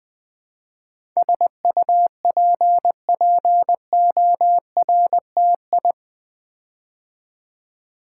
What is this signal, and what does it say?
Morse code, heard as a keyed tone, about 20 wpm, 700 Hz. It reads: SUPPORTI
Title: Build A Word - 4 Letters At A Time - 10-Letter Words - 20wpm